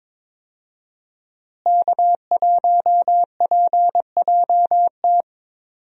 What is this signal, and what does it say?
Morse code, heard as a keyed tone, about 21 wpm, 700 Hz. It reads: K1PJT
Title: USA Call Signs - 5 Characters - Repeated 3x - 22wpm